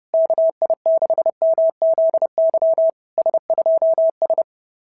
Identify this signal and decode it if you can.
Morse code, heard as a keyed tone, about 26 wpm, 660 Hz. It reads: KI6MZY S2S
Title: SOTA QSO Practice 30wpm